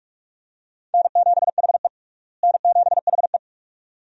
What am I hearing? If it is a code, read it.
Morse code, heard as a keyed tone, about 47 wpm, 700 Hz. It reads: N7HE N7HE
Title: USA Call Signs - 4 Characters - Repeated 2x - 45wpm